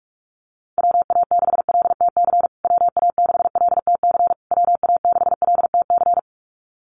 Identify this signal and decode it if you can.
Morse code, heard as a keyed tone, about 44 wpm, 700 Hz. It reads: WA6LTC WA6LTC WA6LTC
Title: USA Call Signs Repeated 3x 45wpm